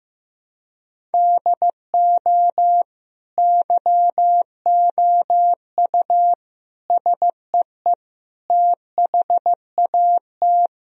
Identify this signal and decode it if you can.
Morse code, heard as a keyed tone, about 15 wpm, 700 Hz. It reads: DO YOU SEE THAT
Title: Sentences from Top 1000 Words - Review Part 1 - 15wpm